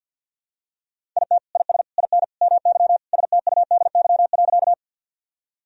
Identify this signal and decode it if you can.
Morse code, heard as a keyed tone, about 47 wpm, 700 Hz. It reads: IT IS IN MY STUDY.
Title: Sentences from Top 1000 Words - Review Part 1 - 50wpm